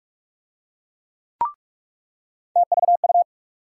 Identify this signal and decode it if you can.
Morse code, heard as a keyed tone, about 46 wpm, 700 Hz. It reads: TVU
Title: USA Call Sign Postfixes - 3 Letters 45wpm